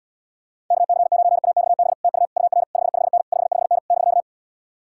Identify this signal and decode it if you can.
Morse code, heard as a keyed tone, about 77 wpm, 700 Hz. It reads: KC9AXB TU UR 55N 55N <BK>